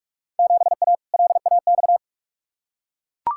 Morse code, heard as a keyed tone, about 42 wpm, 700 Hz. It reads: 7A LAX
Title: Field Day Exchanges Encore - 45wpm